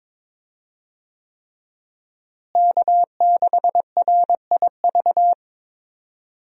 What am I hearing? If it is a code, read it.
Morse code, heard as a keyed tone, about 21 wpm, 700 Hz. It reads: K6RIV
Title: USA Call Signs - 5 Characters - Repeated 3x - 22wpm